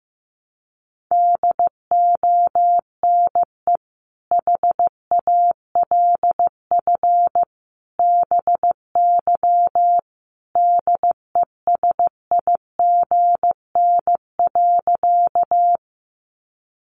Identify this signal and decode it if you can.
Morse code, heard as a keyed tone, about 15 wpm, 700 Hz. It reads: DONE HALF BY DESIGN.
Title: Sentences from Top 800 Words 15wpm